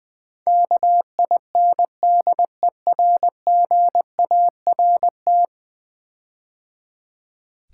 Morse code, heard as a keyed tone, about 19 wpm, 700 Hz. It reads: KINDERGART
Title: Build A Word - 5 Letters At A Time - 12-Letter Words - 20wpm